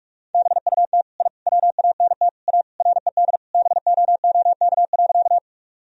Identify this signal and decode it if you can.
Morse code, heard as a keyed tone, about 46 wpm, 700 Hz. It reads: BUT I WANT A RED BOOK.
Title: Sentences from Top 1000 Words - Review Part 2 - 45wpm